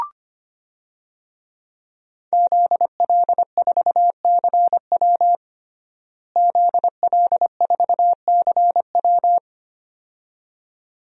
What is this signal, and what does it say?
Morse code, heard as a keyed tone, about 25 wpm, 700 Hz. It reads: ZL4CW ZL4CW
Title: DXCC-100 Call Signs - 5 Characters - Repeated 2x - 25wpm